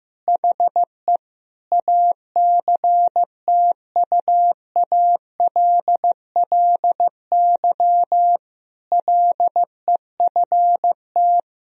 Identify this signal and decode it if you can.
Morse code, heard as a keyed tone, about 15 wpm, 700 Hz. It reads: HE ACTUALLY LEFT
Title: Sentences from Top 1000 Words - Review Part 4 - 15wpm